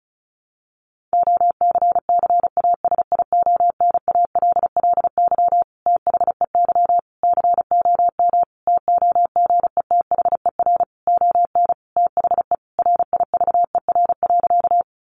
Morse code, heard as a keyed tone, about 34 wpm, 700 Hz. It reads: OCCASIONALLY THEY COM TOGETHER ON THE RIVER.
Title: Sentences from Top 1300 Words 35wpm